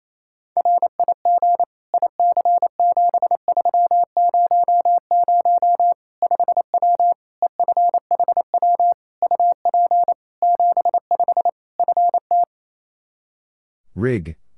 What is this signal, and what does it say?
Morse code, heard as a keyed tone, about 28 wpm, 700 Hz. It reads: RIG IC7300 5W EFHW UP 75 FT
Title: Practice QSO - Exchange 2 - Info - 28wpm